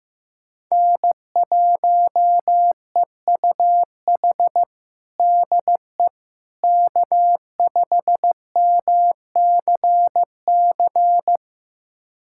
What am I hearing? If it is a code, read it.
Morse code, heard as a keyed tone, about 15 wpm, 700 Hz. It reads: N1EUH DE K5MCC
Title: USA Callsign DE Callsign - 5 Characters - 15wpm